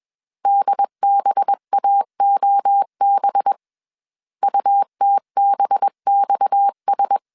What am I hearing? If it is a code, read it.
Morse code, heard as a keyed tone, about 21 wpm, 800 Hz. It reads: DBAOB UTBXS
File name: CW / DET / 1 6 1500C S A 20wpm 2048